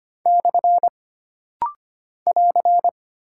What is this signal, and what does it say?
Morse code, heard as a keyed tone, about 25 wpm, 700 Hz. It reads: / <AR>
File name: Single Punctuation-Prosign - Rapid-Fire - 25wpm